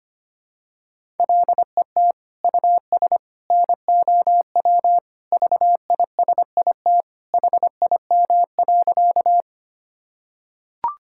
Morse code, heard as a keyed tone, about 25 wpm, 700 Hz. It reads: LET US NOW VISIT HIM.
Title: Sentences from Top 1000 Words - Review Part 6 - 25wpm